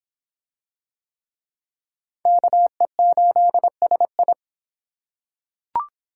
Morse code, heard as a keyed tone, about 24 wpm, 700 Hz. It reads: KE8SI